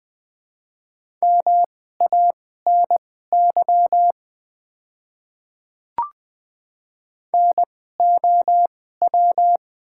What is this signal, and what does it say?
Morse code, heard as a keyed tone, about 20 wpm, spaced as for 14 wpm, 700 Hz. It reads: MANY NOW
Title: 2x Character Spacing - Top 100 Words - 20wpm